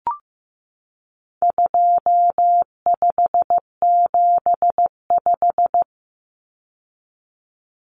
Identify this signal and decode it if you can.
Morse code, heard as a keyed tone, about 15 wpm, 700 Hz. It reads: 2575
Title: Numbers - 4-Digit 15wpm